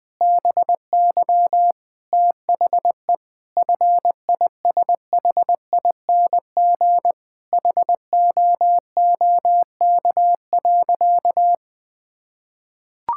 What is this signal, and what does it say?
Morse code, heard as a keyed tone, about 21 wpm, 700 Hz. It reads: BY THE FISHING HOOK.
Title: Sentences from Top 1600 Words 20wpm